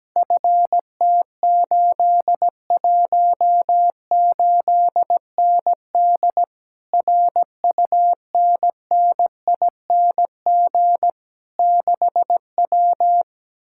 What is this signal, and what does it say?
Morse code, heard as a keyed tone, about 17 wpm, 700 Hz. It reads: FT818ND RUNNING 6W
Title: Speed-Racing Common QSO Phrases - 17wpm